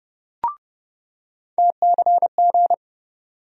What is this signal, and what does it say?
Morse code, heard as a keyed tone, about 30 wpm, 700 Hz. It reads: TCG
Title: USA Call Sign Postfixes - 3 Letters 30wpm